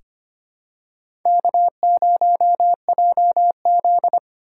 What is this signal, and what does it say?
Morse code, heard as a keyed tone, about 25 wpm, 700 Hz. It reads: K0JZ
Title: USA Call Signs - 4 Characters - 2x Character Spacing - 25wpm